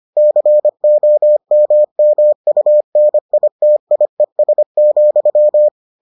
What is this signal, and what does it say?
Morse code, heard as a keyed tone, about 25 wpm, 600 Hz. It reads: COMMUNITIES,